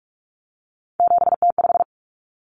Morse code, heard as a keyed tone, about 44 wpm, 700 Hz. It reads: 7T5
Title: Common Cut Numbers - 3-Digit 45wpm